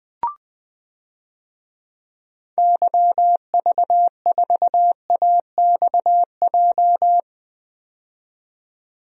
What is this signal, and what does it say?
Morse code, heard as a keyed tone, about 20 wpm, 700 Hz. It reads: YV4AXJ
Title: DXCC-100 Call Signs - 6 Characters - 20wpm